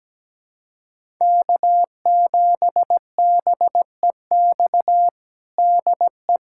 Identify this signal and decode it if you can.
Morse code, heard as a keyed tone, about 17 wpm, 700 Hz. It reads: K7BEX DE
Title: USA Callsign DE Callsign - 5 Characters - 17wpm